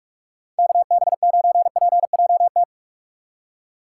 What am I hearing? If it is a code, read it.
Morse code, heard as a keyed tone, about 46 wpm, 700 Hz. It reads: KB9PJT
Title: USA Call Signs - 6 Characters - 4x Character Spacing - 45wpm